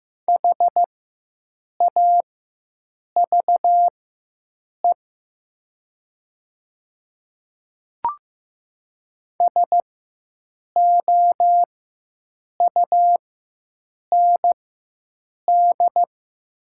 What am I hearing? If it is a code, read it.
Morse code, heard as a keyed tone, about 15 wpm, spaced as for 7 wpm, 700 Hz. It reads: HAVE SOUND
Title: 4x Character Spacing - Top 200 Words - 15wpm